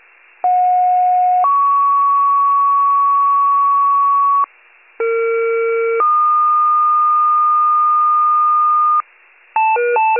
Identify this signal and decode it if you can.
Two-tone page, heard as a one-second tone, then a three-second tone, about 730 then 1120 Hz, about 460 then 1220 Hz. It is heard from 0.4 to 9.0 s.
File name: pager / 21 RESCUE / wcalert